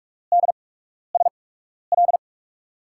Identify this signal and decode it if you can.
Morse code, heard as a keyed tone, about 44 wpm, spaced as for 12 wpm, 700 Hz. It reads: DSL
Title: USA Call Signs - 8x Character Spacing - 45wpm